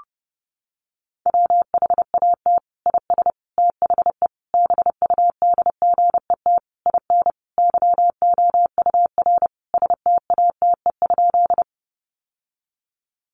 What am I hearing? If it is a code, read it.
Morse code, heard as a keyed tone, about 28 wpm, 700 Hz. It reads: WHAT IS THE BUDGET IN YOUR STATE?
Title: Sentences from Top 1200 Words 30wpm